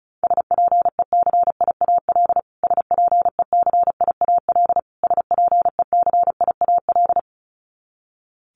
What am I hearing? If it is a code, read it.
Morse code, heard as a keyed tone, about 34 wpm, 700 Hz. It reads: SPECIAL SPECIAL SPECIAL
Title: Top 500 Words Repeated 35wpm